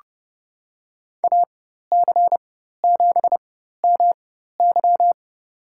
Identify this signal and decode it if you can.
Morse code, heard as a keyed tone, about 29 wpm, spaced as for 14 wpm, 700 Hz. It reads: AC7MY
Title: USA Call Signs - 5 Characters - 4x Character Spacing - 30wpm